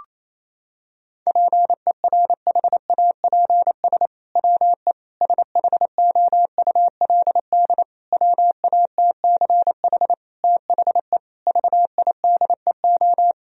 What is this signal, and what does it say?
Morse code, heard as a keyed tone, about 28 wpm, 700 Hz. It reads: PERHAPS WE SHOULD WATCH THE VIDEO
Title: Sentences from Top 1000 Words - Review Part 4 - 28wpm